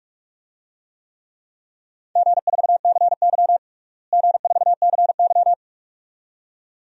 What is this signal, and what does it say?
Morse code, heard as a keyed tone, about 46 wpm, 700 Hz. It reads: G4CY G4CY
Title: DXCC-100 Call Signs - 4 Characters - Repeated 2x - 45wpm